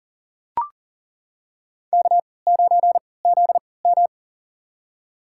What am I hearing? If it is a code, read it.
Morse code, heard as a keyed tone, about 40 wpm, spaced as for 23 wpm, 700 Hz. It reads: K9ZM
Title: USA Call Signs - 4 Characters - 3x Character Spacing - 40wpm